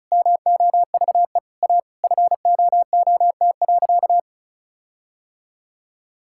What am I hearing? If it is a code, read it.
Morse code, heard as a keyed tone, about 36 wpm, 700 Hz. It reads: MOVE A FOOT.
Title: Sentences from Top 1600 Words 35wpm